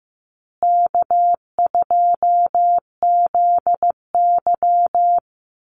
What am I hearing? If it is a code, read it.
Morse code, heard as a keyed tone, about 15 wpm, 700 Hz. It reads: K2ZY
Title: USA Call Signs Repeated 2x 15wpm